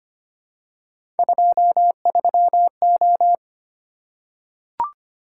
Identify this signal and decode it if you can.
Morse code, heard as a keyed tone, about 25 wpm, 700 Hz. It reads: 23O
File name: Three Letter-Number Combinations - 25wpm